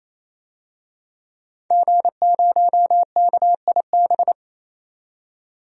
Morse code, heard as a keyed tone, about 28 wpm, 700 Hz. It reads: G0KIB
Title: DXCC-100 Call Signs - 5 Characters - 28wpm